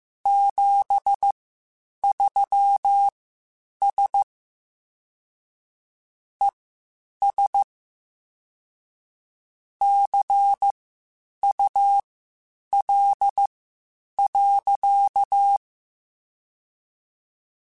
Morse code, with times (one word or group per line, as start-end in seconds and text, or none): 0.3-4.2 73S
6.4-7.6 ES
9.8-15.6 CUL.